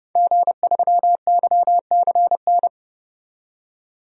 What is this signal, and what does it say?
Morse code, heard as a keyed tone, about 30 wpm, 700 Hz. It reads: G3YCN
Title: DXCC-100 Call Signs - 5 Characters - 30wpm